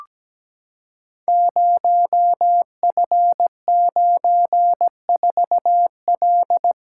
Morse code, heard as a keyed tone, about 17 wpm, 700 Hz. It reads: 0F94L
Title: Code Groups - 5 Letter:Numbers - 17wpm